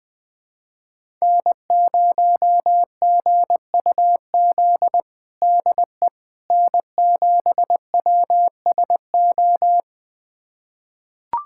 Morse code, heard as a keyed tone, about 21 wpm, 700 Hz. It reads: N0GUZ DE N7WSO
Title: USA Callsign DE Callsign - 5 Characters - 20wpm